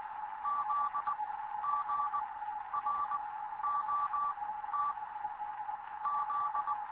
Morse code, heard as a keyed tone, about 20 wpm, 1160 Hz, with noise. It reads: Z G R O T Z
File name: Morze-HA2MN / hangp / 2 11 GYAKORLAT